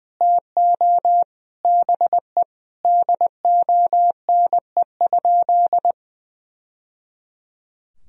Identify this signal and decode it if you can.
Morse code, heard as a keyed tone, about 20 wpm, 700 Hz. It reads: TO BE DONE?
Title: Sentences from Top 1000 Words - Review Part 4 - 20wpm